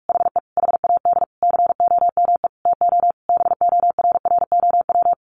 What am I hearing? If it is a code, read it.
Morse code, heard as a keyed tone, about 44 wpm, 700 Hz. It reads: HE HAD COME TO BORROW
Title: Sentences from Top 1500 Words 45wpm